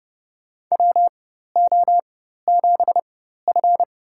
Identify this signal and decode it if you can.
Morse code, heard as a keyed tone, about 28 wpm, spaced as for 14 wpm, 700 Hz. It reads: WO7F
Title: USA Call Signs - 4 Characters - 4x Character Spacing - 30wpm